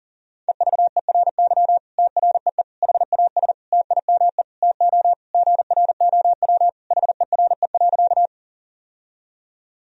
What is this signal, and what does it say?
Morse code, heard as a keyed tone, about 40 wpm, 700 Hz. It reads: EVERY TREE HAS TIME TO GROW HERE.
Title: Sentences from Top 1000 Words - Review Part 1 - 40wpm